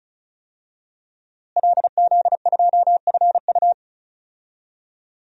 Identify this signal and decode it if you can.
Morse code, heard as a keyed tone, about 34 wpm, 700 Hz. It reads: LZ2FU
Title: DXCC-100 Call Signs - 5 Characters - Repeated 2x - 35wpm